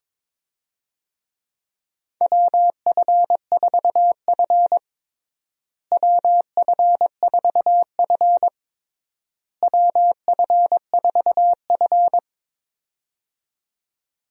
Morse code, heard as a keyed tone, about 22 wpm, 700 Hz. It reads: WF4F WF4F WF4F
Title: USA Call Signs - 4 Characters - Repeated 3x - 22wpm